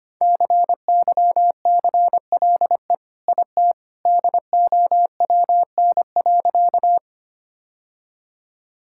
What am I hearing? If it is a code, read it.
Morse code, heard as a keyed tone, about 25 wpm, 700 Hz. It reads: CYCLE IT DOWN.
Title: Sentences from Top 1000 Words - Review Part 6 - 25wpm